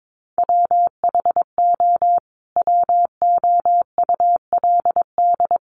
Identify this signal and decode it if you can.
Morse code, heard as a keyed tone, about 22 wpm, 700 Hz. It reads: WHO WOULD
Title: Sentences from Top 1500 Words 22wpm